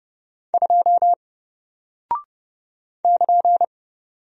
Morse code, heard as a keyed tone, about 30 wpm, 700 Hz. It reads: 2 <KN>
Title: Single Character (Letter:Number:Punctuation:Prosign) Alt 30wpm